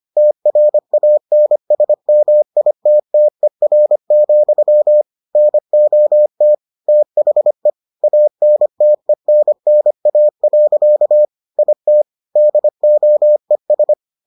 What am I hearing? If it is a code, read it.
Morse code, heard as a keyed tone, about 25 wpm, 600 Hz. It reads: TRANSMITTER, NOT THE ANTENNA. IT DOES